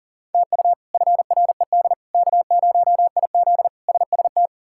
Morse code, heard as a keyed tone, about 40 wpm, 700 Hz. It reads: TU FRED K0IZ SST
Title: K1USN SST QSO Practice - 40wpm